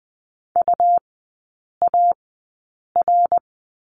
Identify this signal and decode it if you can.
Morse code, heard as a keyed tone, about 20 wpm, spaced as for 8 wpm, 690 Hz. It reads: UAR
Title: 159 Lesson 26e - Letter g - Three Characters 20wpm